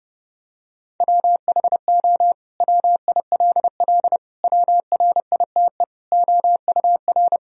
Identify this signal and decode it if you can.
Morse code, heard as a keyed tone, about 30 wpm, 700 Hz. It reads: WHO WILL WRITE OUR
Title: Sentences from Top 1000 Words - Review Part 3 - 30wpm